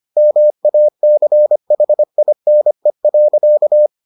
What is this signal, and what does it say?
Morse code, heard as a keyed tone, about 25 wpm, 600 Hz. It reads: MACHINE.